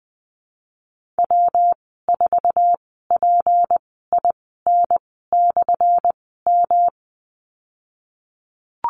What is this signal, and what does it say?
Morse code, heard as a keyed tone, about 20 wpm, spaced as for 15 wpm, 700 Hz. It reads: W4PIN/M
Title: 179 Lesson 29g - Slash - Call Signs 20wpm